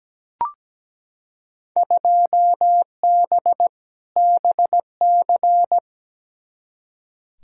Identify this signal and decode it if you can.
Morse code, heard as a keyed tone, about 17 wpm, 700 Hz. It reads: 2B BC
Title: Field Day Exchanges Encore - 17wpm